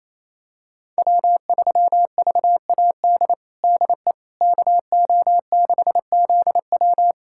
Morse code, heard as a keyed tone, about 28 wpm, 700 Hz. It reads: W3VAD DE KO6ZW
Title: USA Callsign DE Callsign - 5 Characters - 28wpm